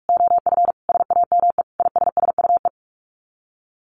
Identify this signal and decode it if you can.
Morse code, heard as a keyed tone, about 44 wpm, 700 Hz. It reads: OF SAME ISSUE
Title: Sets of 5 Words Encore2 45wpm